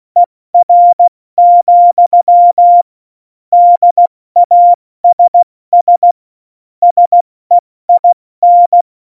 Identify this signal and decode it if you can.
Morse code, heard as a keyed tone, about 16 wpm, 700 Hz. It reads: ER, DASS SEIN
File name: Morsekurs / Witze / Witz 26 Mai, 2020 70 F80